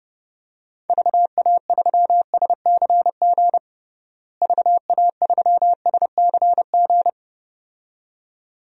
Morse code, heard as a keyed tone, about 30 wpm, 700 Hz. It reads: VA3SCG VA3SCG
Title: DXCC-100 Call Signs - 6 Characters - Repeated 2x - 30wpm